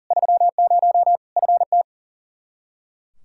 Morse code, heard as a keyed tone, about 37 wpm, 700 Hz. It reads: 30 FT
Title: Practice QSO - Exchange 2 - Info - 40wpm